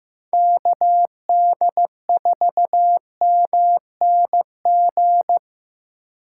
Morse code, heard as a keyed tone, about 18 wpm, 700 Hz. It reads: KD4MNG
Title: Speed Racing - USA 2x3 Call Signs 15wpm